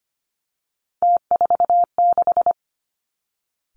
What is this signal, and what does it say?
Morse code, heard as a keyed tone, about 25 wpm, 700 Hz. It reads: T46
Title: Three Challenging Letter-Number Combinations 25wpm